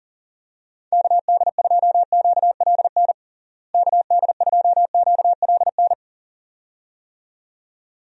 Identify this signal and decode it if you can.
Morse code, heard as a keyed tone, about 40 wpm, 700 Hz. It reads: KD2QLN KD2QLN
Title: USA Call Signs - 6 Characters - Repeated 2x - 40wpm